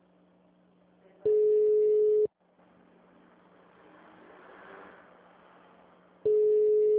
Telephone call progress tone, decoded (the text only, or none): ringback tone